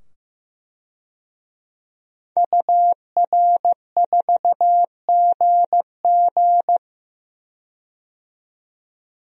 Morse code, heard as a keyed tone, about 15 wpm, 700 Hz. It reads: UR4GG
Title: DXCC-100 Call Signs - 5 Characters - 15wpm